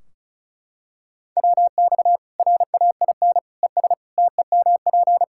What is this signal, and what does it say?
Morse code, heard as a keyed tone, about 36 wpm, 700 Hz. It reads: WX RAIN ES TEMP
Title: Practice QSO - Exchange 2 - Info - 35wpm